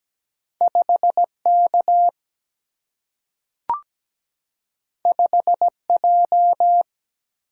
Morse code, heard as a keyed tone, about 17 wpm, 700 Hz. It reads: 5 K 5 J
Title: Single-Letter Single-Number 17wpm